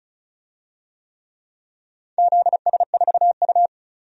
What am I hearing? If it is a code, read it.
Morse code, heard as a keyed tone, about 36 wpm, 700 Hz. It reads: ZS4U
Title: DXCC-100 Call Signs - 4 Characters - 35wpm